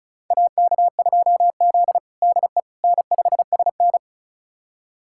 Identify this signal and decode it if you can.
Morse code, heard as a keyed tone, about 36 wpm, 700 Hz. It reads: AK2Z DE N5SN
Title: USA Callsign DE Callsign - 4 Characters - 35wpm